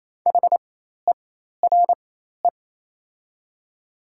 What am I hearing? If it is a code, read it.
Morse code, heard as a keyed tone, about 28 wpm, spaced as for 13 wpm, 700 Hz. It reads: HERE
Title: 4x Character Spacing - Top 500 Words - 28wpm